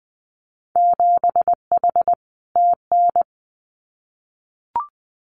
Morse code, heard as a keyed tone, about 20 wpm, 700 Hz. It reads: 7H TN
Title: Winter Field Day Exchanges Encore 20wpm